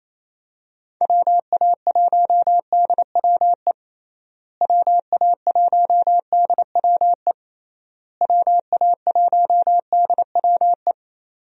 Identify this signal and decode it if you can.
Morse code, heard as a keyed tone, about 28 wpm, 700 Hz. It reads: WA1DWE WA1DWE WA1DWE
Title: USA Call Signs - 6 Characters - Repeated 3x - 28wpm